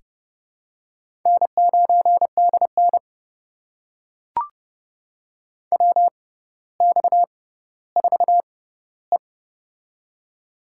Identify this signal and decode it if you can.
Morse code, heard as a keyed tone, about 30 wpm, spaced as for 15 wpm, 700 Hz. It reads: N9DN WX4E
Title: USA Call Signs - 4 Characters - 6x Character Spacing - 30wpm